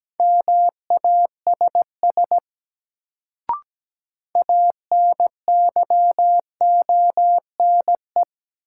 Morse code, heard as a keyed tone, about 17 wpm, 700 Hz. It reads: MASS ANYONE
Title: Sets of 5 Words Encore - New Word Drill - 17wpm